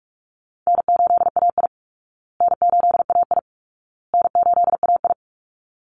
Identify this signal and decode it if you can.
Morse code, heard as a keyed tone, about 44 wpm, 700 Hz. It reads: N8AI N8AI N8AI
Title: Call Signs Repeated 2x - 45wpm